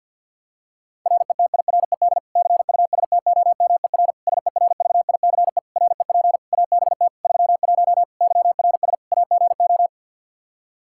Morse code, heard as a keyed tone, about 46 wpm, 700 Hz. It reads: RETIRED CUSTOMER SERVICE REP ABT 31 YRS AGO
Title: Practice QSO - Exchange 3 - More Info - 50wpm